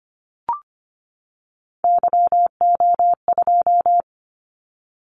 Morse code, heard as a keyed tone, about 25 wpm, 700 Hz. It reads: YO2
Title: Callsign Prefixes for Top 25 DXCC - 25wpm